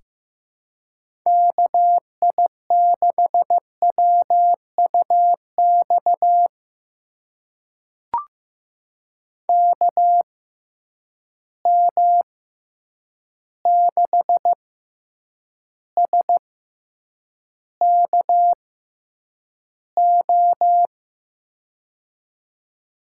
Morse code, heard as a keyed tone, about 15 wpm, spaced as for 8 wpm, 700 Hz. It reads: KI6WUX KM6SKO
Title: USA Call Signs - 6x Character Spacing - 15wpm